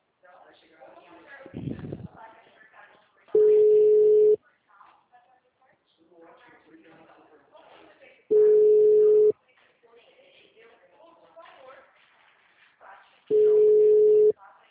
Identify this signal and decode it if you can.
Ringback tone (Europe, 425 Hz), with speech.
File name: out- -2010- - - .23152